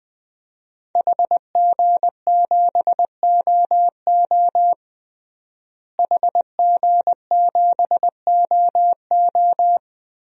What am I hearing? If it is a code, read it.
Morse code, heard as a keyed tone, about 20 wpm, 700 Hz. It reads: HG7OO HG7OO
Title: DXCC-100 Call Signs - 5 Characters - Repeated 3x - 20wpm